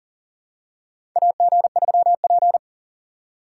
Morse code, heard as a keyed tone, about 37 wpm, 700 Hz. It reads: AG3P
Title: USA Call Signs - 4 Characters - 6x Character Spacing - 40wpm